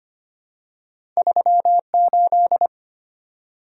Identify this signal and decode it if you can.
Morse code, heard as a keyed tone, about 25 wpm, 700 Hz. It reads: 38
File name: Two Letter-Number Combinations - 25wpm